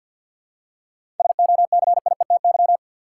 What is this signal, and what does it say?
Morse code, heard as a keyed tone, about 52 wpm, 700 Hz. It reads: SOCIETY
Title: Seven Letter Words - 50wpm